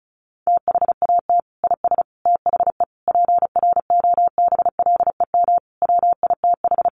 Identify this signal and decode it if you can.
Morse code, heard as a keyed tone, about 34 wpm, 700 Hz. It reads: THAT IS THE PROBLEM WITH
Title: Sentences from Top 1100 Words 35wpm